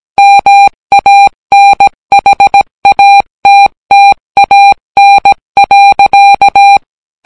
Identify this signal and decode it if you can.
Morse code, heard as a keyed tone, about 17 wpm, 800 Hz. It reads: MANHATTAN.